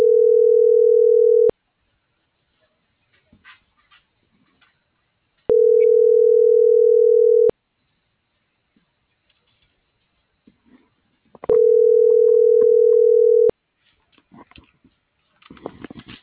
Ringback tone (North America), 440 + 480 Hz.